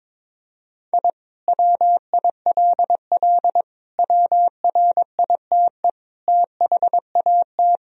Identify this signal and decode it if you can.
Morse code, heard as a keyed tone, about 22 wpm, 700 Hz. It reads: I WILL WRITE THAT
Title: Sentences from Top 1000 Words - Review Part 3 - 22wpm